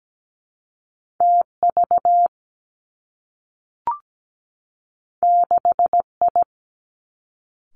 Morse code, heard as a keyed tone, about 17 wpm, 700 Hz. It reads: T V 6 I
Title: Two Challenging Letter-Number Combinations 17wpm